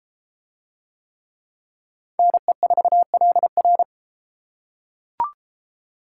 Morse code, heard as a keyed tone, about 31 wpm, 700 Hz. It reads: NE4LR